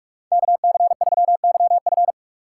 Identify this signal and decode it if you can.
Morse code, heard as a keyed tone, about 47 wpm, 700 Hz. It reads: KC3YF
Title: USA Call Signs - 5 Characters - 6x Character Spacing - 45wpm